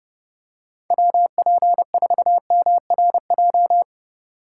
Morse code, heard as a keyed tone, about 30 wpm, 700 Hz. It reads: WP4MRJ